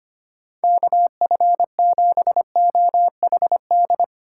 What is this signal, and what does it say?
Morse code, heard as a keyed tone, about 32 wpm, 700 Hz. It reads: KF7OHD